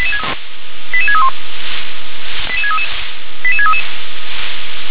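Five-tone selective call (ZVEI), heard as quick runs of five tones, tones of about 70 ms. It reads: 88631 88633 88633
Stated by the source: ZVEI five-tone selective call received on an SDR.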